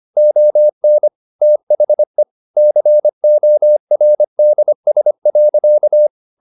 Morse code, heard as a keyed tone, about 25 wpm, 600 Hz. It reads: ON THE CORDS.
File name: the old man and the sea part 1 25wpm